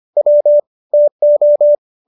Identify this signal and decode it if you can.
Morse code, heard as a keyed tone, about 25 wpm, 590 Hz. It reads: W TO
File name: qso206 25